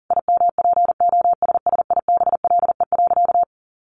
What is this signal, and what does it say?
Morse code, heard as a keyed tone, about 40 wpm, 700 Hz. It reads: IMPOSSIBLE.
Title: Sentences from Top 1100 Words 40wpm